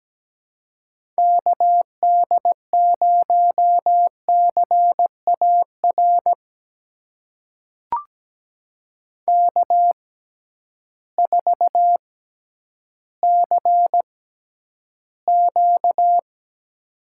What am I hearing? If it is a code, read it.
Morse code, heard as a keyed tone, about 17 wpm, spaced as for 10 wpm, 700 Hz. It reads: KD0CAR K4CQ
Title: USA Call Signs - 6x Character Spacing - 17wpm